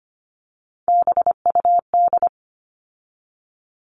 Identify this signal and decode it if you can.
Morse code, heard as a keyed tone, about 25 wpm, 700 Hz. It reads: BUD
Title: All Cut Numbers - 3-Digit 25wpm